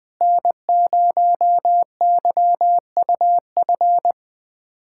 Morse code, heard as a keyed tone, about 20 wpm, 700 Hz. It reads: N0YUF
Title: USA Callsign DE Callsign - 5 Characters - 20wpm